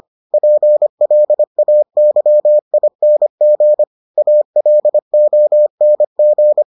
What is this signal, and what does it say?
Morse code, heard as a keyed tone, about 25 wpm, 600 Hz. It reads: PLAYING ALONG